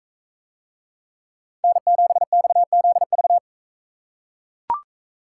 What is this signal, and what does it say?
Morse code, heard as a keyed tone, about 40 wpm, 700 Hz. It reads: N7XZV